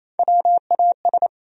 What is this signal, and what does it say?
Morse code, heard as a keyed tone, about 28 wpm, 700 Hz. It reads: WAS
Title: Top 100 Words 28wpm